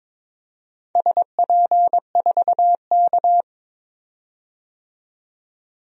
Morse code, heard as a keyed tone, about 22 wpm, 700 Hz. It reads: SP4K
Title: DXCC-100 Call Signs - 4 Characters - 22wpm